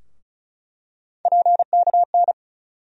Morse code, heard as a keyed tone, about 34 wpm, 700 Hz. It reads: PKN
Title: USA Call Sign Postfixes - 3 Letters 35wpm